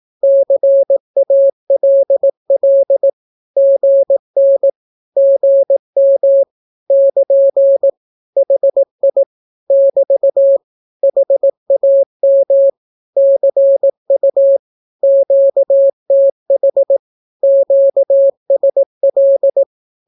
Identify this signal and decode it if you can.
Morse code, heard as a keyed tone, about 18 wpm, 550 Hz. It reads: CALL GN GM <KN> HI <BT> HAM CU QTH QSL
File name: naukapodstaw / skroty / skroty